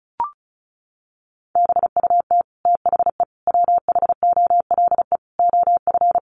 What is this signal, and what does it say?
Morse code, heard as a keyed tone, about 34 wpm, 700 Hz. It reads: BUT THE WHOLE OF
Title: Sentences from Top 1500 Words 35wpm